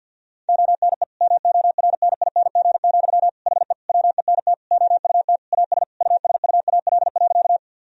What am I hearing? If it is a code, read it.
Morse code, heard as a keyed tone, about 52 wpm, 700 Hz. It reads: ONE MORNING, HE WENT OUT AS USUAL.
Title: Sentences from Top 1000 Words - Review Part 2 - 50wpm